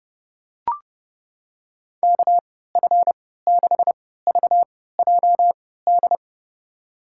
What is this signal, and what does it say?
Morse code, heard as a keyed tone, about 32 wpm, spaced as for 17 wpm, 700 Hz. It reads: KF6VJD
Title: USA Call Signs - 6 Characters - 3x Character Spacing - 30wpm